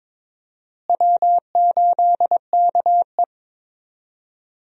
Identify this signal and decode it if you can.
Morse code, heard as a keyed tone, about 22 wpm, 700 Hz. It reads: W8KE
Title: USA Call Signs - 4 Characters - Repeated 3x - 22wpm